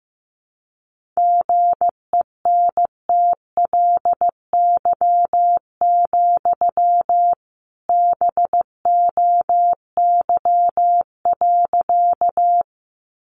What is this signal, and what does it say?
Morse code, heard as a keyed tone, about 15 wpm, 700 Hz. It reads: GENTLY, BOY.
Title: Sentences from Top 1500 Words 15wpm